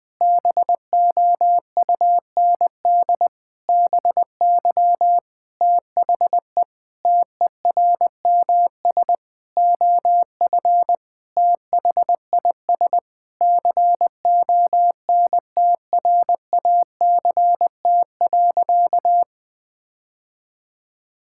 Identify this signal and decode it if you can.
Morse code, heard as a keyed tone, about 20 wpm, 700 Hz. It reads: BOUND BY THE TERMS OF THIS CONTRACT.
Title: Sentences with 9-Word Phrases - 20wpm